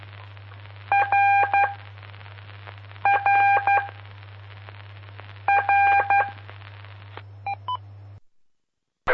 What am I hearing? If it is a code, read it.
Morse code, heard as a keyed tone, about 12 wpm, spaced as for 5 wpm, 800 Hz. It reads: RRRE
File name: pp5bar / aulas de telegrafia / Aula-04